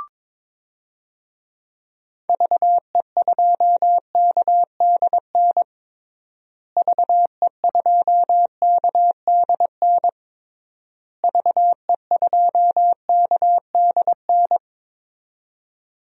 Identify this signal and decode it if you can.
Morse code, heard as a keyed tone, about 22 wpm, 700 Hz. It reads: VE2KDN VE2KDN VE2KDN
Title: DXCC-100 Call Signs - 6 Characters - Repeated 3x - 22wpm